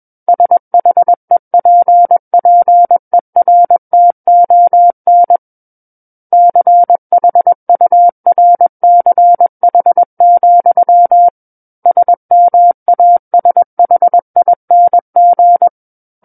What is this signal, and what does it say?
Morse code, heard as a keyed tone, about 21 wpm, 700 Hz. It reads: SHEPPERTON CHURCH, SMASHING
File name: Book One / 12-Chapter 12 - 21 WPM